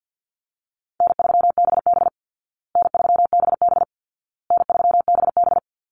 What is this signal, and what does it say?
Morse code, heard as a keyed tone, about 50 wpm, 700 Hz. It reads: N3BB N3BB N3BB
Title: Easy Call Signs Repeated 3x - 50wpm